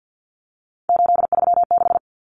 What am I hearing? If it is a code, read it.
Morse code, heard as a keyed tone, about 50 wpm, 700 Hz. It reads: 836
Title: Numbers - 3-Digit 50wpm